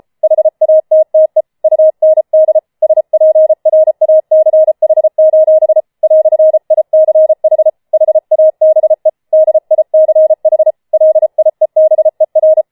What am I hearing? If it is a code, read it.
Morse code, heard as a keyed tone, about 33 wpm, 630 Hz. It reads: HATTE UND SPRACH: "ICH HABE DICH LIEBER
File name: Schneewittchen Teil5 34wpm